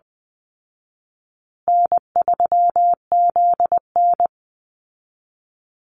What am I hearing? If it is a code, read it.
Morse code, heard as a keyed tone, about 20 wpm, 700 Hz. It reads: N3ZN
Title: Call Signs Repeated 2x - 20wpm